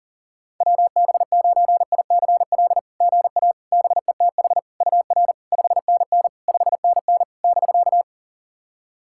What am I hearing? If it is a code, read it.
Morse code, heard as a keyed tone, about 40 wpm, 690 Hz. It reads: WB9ICL GA BETH UR 5NN 5NN <BK>
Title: SOTA QSO Practice 40wpm